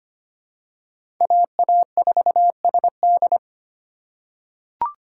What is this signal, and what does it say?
Morse code, heard as a keyed tone, about 25 wpm, 700 Hz. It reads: AA4SD
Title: USA Call Signs - 5 Characters - 2x Character Spacing - 25wpm